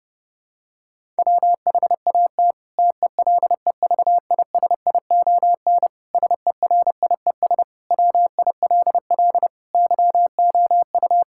Morse code, heard as a keyed tone, about 30 wpm, 700 Hz. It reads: WHAT TELEVISION SERIES WILL YOU
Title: Sentences from Top 1000 Words - Review Part 5 - 30wpm